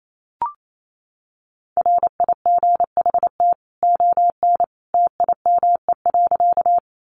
Morse code, heard as a keyed tone, about 28 wpm, 700 Hz. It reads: RIGHT ON TIME.
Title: A Sound Of Thunder 28wpm